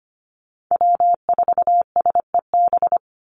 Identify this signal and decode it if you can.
Morse code, heard as a keyed tone, about 25 wpm, 700 Hz. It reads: W4SEB
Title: USA Call Signs 25wpm